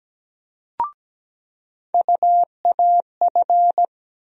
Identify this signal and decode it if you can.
Morse code, heard as a keyed tone, about 17 wpm, 700 Hz. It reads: UAF
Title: USA Call Sign Postfixes - 3 Letters 17wpm